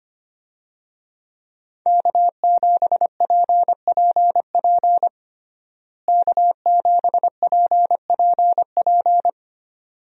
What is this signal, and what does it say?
Morse code, heard as a keyed tone, about 25 wpm, 700 Hz. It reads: K7PPP K7PPP
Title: USA Call Signs - 5 Characters - Repeated 2x - 25wpm